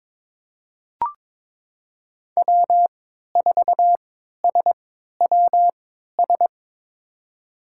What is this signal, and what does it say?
Morse code, heard as a keyed tone, about 21 wpm, spaced as for 12 wpm, 700 Hz. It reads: W4SWS
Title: USA Call Signs - 5 Characters - 3x Character Spacing - 22wpm